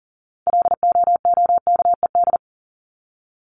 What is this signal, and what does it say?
Morse code, heard as a keyed tone, about 40 wpm, 700 Hz. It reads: LOOKED
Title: ED Suffix - Top 100 Words with Base Words 40wpm